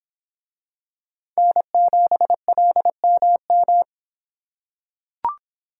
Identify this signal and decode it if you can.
Morse code, heard as a keyed tone, about 25 wpm, 700 Hz. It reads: N7LMM